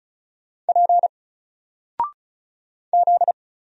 Morse code, heard as a keyed tone, about 35 wpm, 700 Hz. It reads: P Z
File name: Single-Letter-Number-35wpm